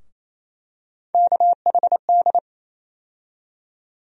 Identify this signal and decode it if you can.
Morse code, heard as a keyed tone, about 28 wpm, 700 Hz. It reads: KHD
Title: USA Call Sign Postfixes - 3 Letters 28wpm